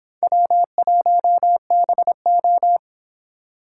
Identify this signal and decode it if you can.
Morse code, heard as a keyed tone, about 26 wpm, 700 Hz. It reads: W1BO